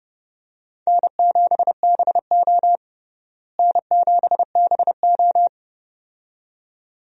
Morse code, heard as a keyed tone, about 32 wpm, 700 Hz. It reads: N7BO N7BO
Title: USA Call Signs - 4 Characters - Repeated 2x - 30wpm